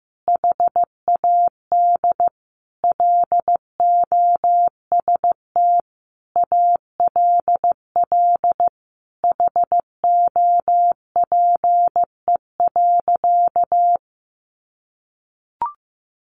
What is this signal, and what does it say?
Morse code, heard as a keyed tone, about 15 wpm, 700 Hz. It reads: HAD LOST ALL HOPE.
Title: Sentences from Top 1400 Words 15wpm